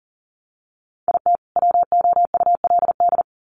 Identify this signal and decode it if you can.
Morse code, heard as a keyed tone, about 40 wpm, 700 Hz. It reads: IT WOULD
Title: Sets of 2 Words 40wpm